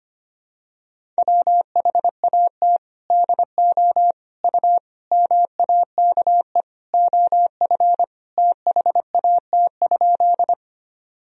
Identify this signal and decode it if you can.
Morse code, heard as a keyed tone, about 25 wpm, 700 Hz. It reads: WHAT DO U MAKE OF THAT?
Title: Sentences from Top 100 Words and Abbreviations 25wpm